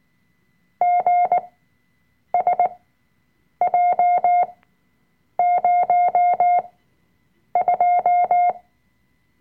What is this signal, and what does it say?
Morse code, heard as a keyed tone, about 19 wpm, spaced as for 7 wpm, 710 Hz. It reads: GSJ02
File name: Workshop Dateien / CW-Online-Kurs / Lektion 005